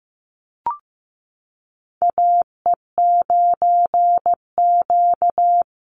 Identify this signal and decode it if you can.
Morse code, heard as a keyed tone, about 15 wpm, 700 Hz. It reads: AE9Q
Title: NA SOTA Chaser Callsigns 15wpm